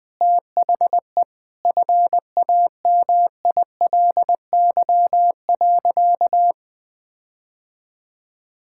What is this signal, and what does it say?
Morse code, heard as a keyed tone, about 21 wpm, 700 Hz. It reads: THE FAMILY.
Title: Sentences with 8-Word Phrases - 20wpm